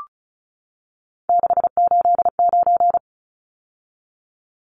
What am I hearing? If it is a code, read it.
Morse code, heard as a keyed tone, about 34 wpm, 700 Hz. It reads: 689
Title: Numbers - 3-Digit 35wpm